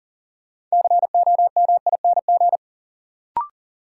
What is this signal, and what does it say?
Morse code, heard as a keyed tone, about 40 wpm, 700 Hz. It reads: COMING
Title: Sentences from Top 1300 Words - New Word Drill - 40wpm